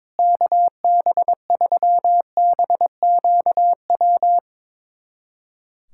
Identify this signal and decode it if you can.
Morse code, heard as a keyed tone, about 22 wpm, 700 Hz. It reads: KB3BQW
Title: USA Callsign DE Callsign - 6 Characters - 22wpm